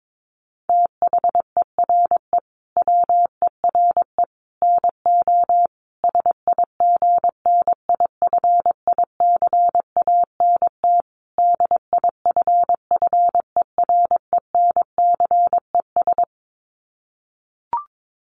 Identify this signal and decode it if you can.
Morse code, heard as a keyed tone, about 22 wpm, 700 Hz. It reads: THERE WERE NO SIGNIFICANT DIFFERENCES
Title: Sets of 5 Words Encore2 22wpm